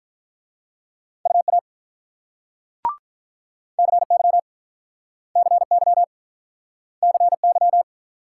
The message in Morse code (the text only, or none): UA CY CY CY